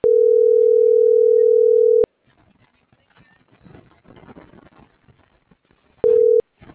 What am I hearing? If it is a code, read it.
Ringback tone (North America), 440 + 480 Hz.